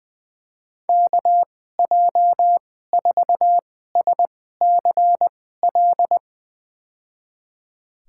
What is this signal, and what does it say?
Morse code, heard as a keyed tone, about 21 wpm, spaced as for 15 wpm, 700 Hz. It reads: KJ4SCL
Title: USA Call Signs - 6 Characters - 2x Character Spacing - 20wpm